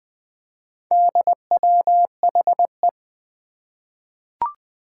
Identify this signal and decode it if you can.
Morse code, heard as a keyed tone, about 20 wpm, 700 Hz. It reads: DWHE
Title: Four Challenging Letter-Number Combinations 20wpm